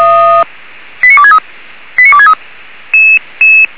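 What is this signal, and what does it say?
Five-tone selective call (ZVEI), heard as quick runs of five tones, tones of about 70 ms. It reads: 89263 89263
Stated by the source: ZVEI five-tone selective call received by SDR.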